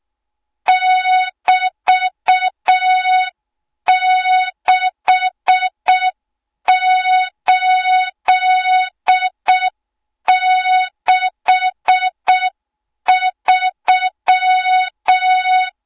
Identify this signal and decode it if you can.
Morse code, heard as a keyed tone, about 6 wpm, 3700 Hz. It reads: <BT>6863
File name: Morze-HA2MN / hang / 20 GYAKORLAT